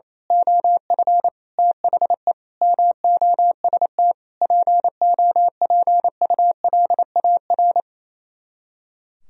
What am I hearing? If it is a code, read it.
Morse code, heard as a keyed tone, about 27 wpm, 700 Hz. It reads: OF THE MOST POPULAR
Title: Speed-Racing - Sets of 5 Words 28wpm